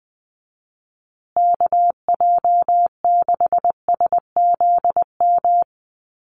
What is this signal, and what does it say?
Morse code, heard as a keyed tone, about 19 wpm, 700 Hz. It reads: KJ6SZM
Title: USA Call Signs Repeated 2x 20wpm